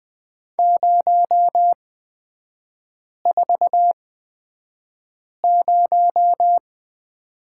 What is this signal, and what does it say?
Morse code, heard as a keyed tone, about 20 wpm, spaced as for 5 wpm, 700 Hz. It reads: 040
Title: 243 Lesson 39e - Letter z - Three Characters 20wpm